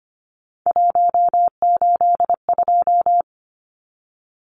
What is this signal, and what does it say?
Morse code, heard as a keyed tone, about 25 wpm, 700 Hz. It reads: 182
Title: Numbers - 3-Digit 25wpm